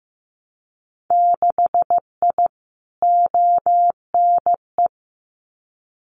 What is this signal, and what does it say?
Morse code, heard as a keyed tone, about 15 wpm, 700 Hz. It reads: 6I ONE
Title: Winter Field Day Exchanges Encore 15wpm